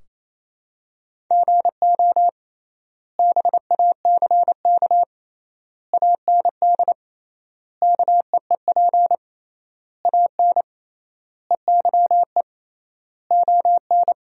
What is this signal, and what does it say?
Morse code, heard as a keyed tone, about 28 wpm, 700 Hz. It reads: GO BACK AND KEEP AN EYE ON
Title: 3x Word-Spacing - Sentences from Top 300 Words - 28wpm